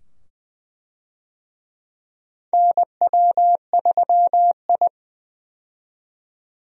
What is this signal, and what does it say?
Morse code, heard as a keyed tone, about 20 wpm, 700 Hz. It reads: NW3I
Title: USA Call Signs - 4 Characters - 20wpm